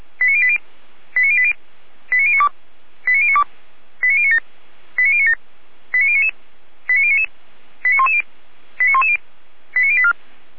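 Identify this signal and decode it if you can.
Five-tone selective call (ZVEI), heard as quick runs of five tones, tones of about 70 ms. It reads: 89080 89080 89092 89092 89097 89097 89099 89099 89110 89110 89084
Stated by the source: Five-tone ZVEI selective call received by SDR, from the feuerwehr-orlamuende.org fire brigade audio server.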